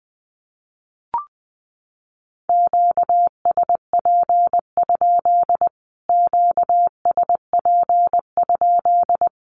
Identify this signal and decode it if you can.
Morse code, heard as a keyed tone, about 20 wpm, 700 Hz. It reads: QSP? QSP?
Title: Q-Codes Repeated 3x 20wpm